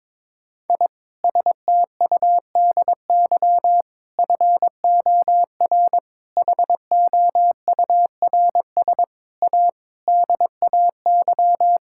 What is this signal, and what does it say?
Morse code, heard as a keyed tone, about 22 wpm, 700 Hz. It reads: I STUDY FOR HOURS A DAY
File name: Sentences with 7-Word Phrases - 22wpm